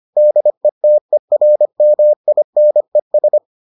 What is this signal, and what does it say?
Morse code, heard as a keyed tone, about 25 wpm, 600 Hz. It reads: DETERMINES